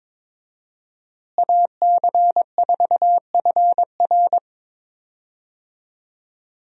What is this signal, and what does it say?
Morse code, heard as a keyed tone, about 22 wpm, 700 Hz. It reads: AC4FR
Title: USA Call Signs - 5 Characters - Repeated 2x - 22wpm